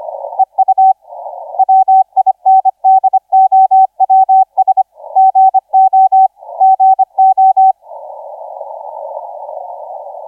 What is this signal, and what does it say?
Morse code, heard as a keyed tone, about 25 wpm, 760 Hz. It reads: EU WINDOWS GO GO
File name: XZ1Z 1425UTC 15m